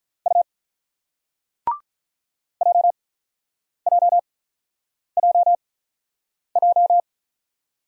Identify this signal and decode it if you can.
Morse code, heard as a keyed tone, about 42 wpm, 700 Hz. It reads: U J J J J
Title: Speed-Racing Single Letters 35wpm